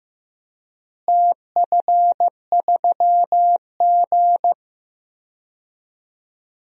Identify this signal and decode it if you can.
Morse code, heard as a keyed tone, about 15 wpm, 700 Hz. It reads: TF3G
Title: DXCC-100 Call Signs - 4 Characters - Repeated 2x - 15wpm